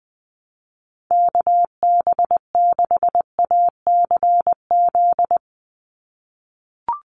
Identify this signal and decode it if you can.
Morse code, heard as a keyed tone, about 20 wpm, 700 Hz. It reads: KB6ACZ
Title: USA Call Signs 20wpm